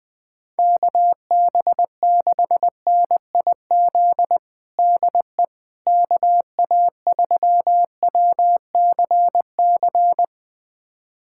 Morse code, heard as a keyed tone, about 20 wpm, 700 Hz. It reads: KB6NIZ DE KA3WCC
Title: USA Callsign DE Callsign - 6 Characters - 20wpm